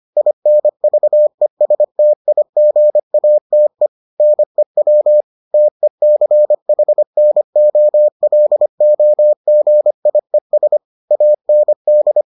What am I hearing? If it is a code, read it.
Morse code, heard as a keyed tone, about 25 wpm, 600 Hz. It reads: INVESTIGATE NEW TECHNOLOGIES AND